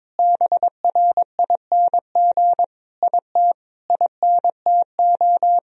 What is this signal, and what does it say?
Morse code, heard as a keyed tone, about 22 wpm, 700 Hz. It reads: BRING IT INTO
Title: Sentences from Top 1000 Words - Review Part 4 - 22wpm